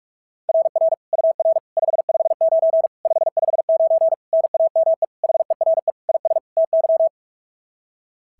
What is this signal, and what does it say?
Morse code, heard as a keyed tone, about 46 wpm, 690 Hz. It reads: RR UR 559 559 NAME HERE IS TY
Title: SOTA QSO Practice 45wpm